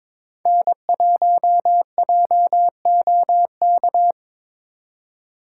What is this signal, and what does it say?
Morse code, heard as a keyed tone, about 22 wpm, 700 Hz. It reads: N1JOK
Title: USA Call Signs - 5 Characters - 3x Character Spacing - 22wpm